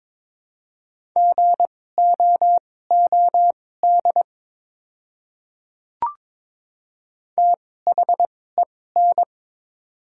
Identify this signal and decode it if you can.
Morse code, heard as a keyed tone, about 22 wpm, spaced as for 16 wpm, 700 Hz. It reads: GOOD THEN
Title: 2x Character Spacing - Top 300 Words - 22wpm